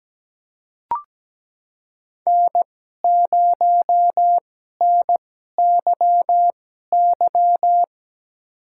Morse code, heard as a keyed tone, about 17 wpm, spaced as for 12 wpm, 700 Hz. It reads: N0NYY
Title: USA Call Signs - 5 Characters - 2x Character Spacing - 17wpm